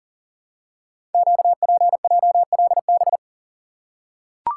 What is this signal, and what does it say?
Morse code, heard as a keyed tone, about 41 wpm, 700 Hz. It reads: QPJLB
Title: Code Groups - 5 Letters - 40wpm